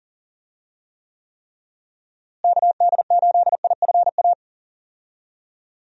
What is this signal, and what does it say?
Morse code, heard as a keyed tone, about 40 wpm, 700 Hz. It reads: KD8IFA
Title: USA Call Signs - 6 Characters - Repeated 3x - 40wpm